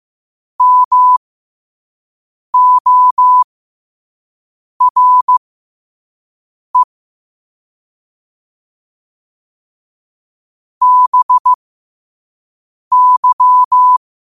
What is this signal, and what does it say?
Morse code, heard as a keyed tone, about 15 wpm, spaced as for 5 wpm, 1000 Hz. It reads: MORE BY
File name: qotdfiles / 5wpm